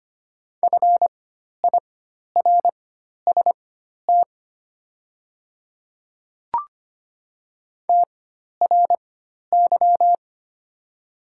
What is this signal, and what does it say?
Morse code, heard as a keyed tone, about 25 wpm, spaced as for 12 wpm, 700 Hz. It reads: FIRST TRY
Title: 4x Character Spacing - Top 200 Words - 25wpm